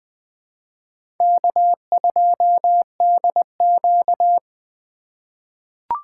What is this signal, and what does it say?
Morse code, heard as a keyed tone, about 20 wpm, 700 Hz. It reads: K2DQ
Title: USA Call Signs - 4 Characters - 2x Character Spacing - 20wpm